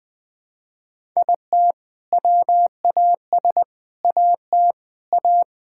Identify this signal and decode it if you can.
Morse code, heard as a keyed tone, about 20 wpm, 700 Hz. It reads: IT WAS AT A
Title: Sentences from Top 1000 Words - Review Part 5 - 20wpm